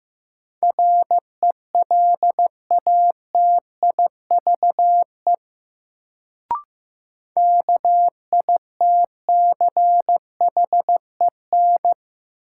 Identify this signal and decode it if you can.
Morse code, heard as a keyed tone, about 15 wpm, 700 Hz. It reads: RELATIVE KITCHEN
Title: Sentences from Top 1400 Words - New Word Drill - 15wpm